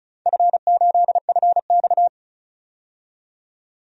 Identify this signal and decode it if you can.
Morse code, heard as a keyed tone, about 36 wpm, 700 Hz. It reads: F8FX
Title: DXCC-100 Call Signs - 4 Characters - 35wpm